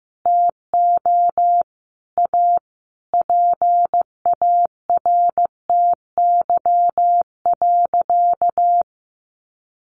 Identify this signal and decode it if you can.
Morse code, heard as a keyed tone, about 15 wpm, 700 Hz. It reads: TO A PARTY.
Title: Easy Phrases 15wpm